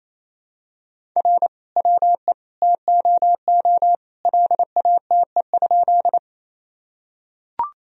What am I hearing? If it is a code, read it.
Morse code, heard as a keyed tone, about 27 wpm, 700 Hz. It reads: R WE TOO LATE?
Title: Sentences from Top 300 Words and Abbreviations 28wpm